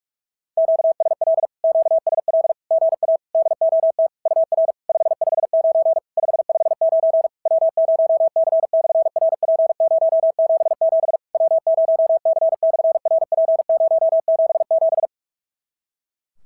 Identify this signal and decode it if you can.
Morse code, heard as a keyed tone, about 47 wpm, 650 Hz. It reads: QSL QSL GA DOT UR 559 559 W0C/RP077 W0C/RP077
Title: SOTA QSO Practice 45wpm